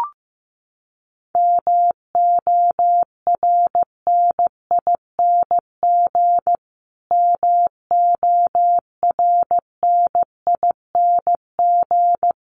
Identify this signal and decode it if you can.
Morse code, heard as a keyed tone, about 15 wpm, 700 Hz. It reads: MORNING MORNING
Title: Top 500 Words Repeated 15wpm